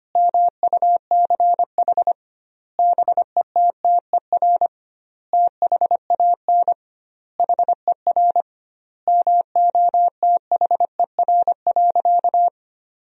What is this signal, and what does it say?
Morse code, heard as a keyed tone, about 25 wpm, 700 Hz. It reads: MUCH BETTER THAN HER MOTHER.
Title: 2x Word-Spacing - Sentences from Top 400 Words - 25wpm